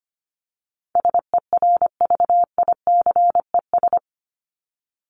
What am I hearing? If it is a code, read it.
Morse code, heard as a keyed tone, about 25 wpm, 700 Hz. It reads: SERVICES
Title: ES:S Suffix - Top 100 Words with Base Words 25wpm